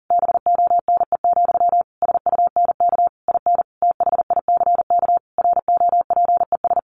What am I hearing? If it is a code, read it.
Morse code, heard as a keyed tone, about 40 wpm, 700 Hz. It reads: BONE, SUNK IN THICK ROPES
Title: A Sound Of Thunder 40wpm